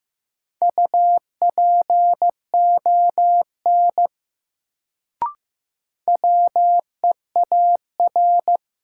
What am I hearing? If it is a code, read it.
Morse code, heard as a keyed tone, about 15 wpm, 700 Hz. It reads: UPON WEAR
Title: Sentences from Top 1000 Words - Review Part 6 - New Word Drill - 15wpm